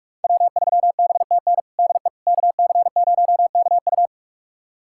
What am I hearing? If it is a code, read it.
Morse code, heard as a keyed tone, about 41 wpm, 700 Hz. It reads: W3BTN DE KC0KU
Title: USA Callsign DE Callsign - 5 Characters - 45wpm